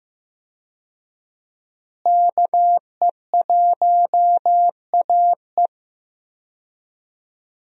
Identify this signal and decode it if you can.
Morse code, heard as a keyed tone, about 15 wpm, 700 Hz. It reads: KE1AE
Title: USA Call Signs - 5 Characters - Repeated 2x - 15wpm